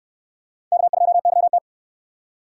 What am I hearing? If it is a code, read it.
Morse code, heard as a keyed tone, about 64 wpm, 700 Hz. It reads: K2YT